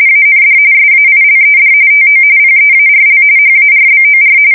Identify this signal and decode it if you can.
RTTY, about 45.45 baud. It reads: AND SAILED TO PEARL HARBOR
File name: fsk170 / USS-New-Jersey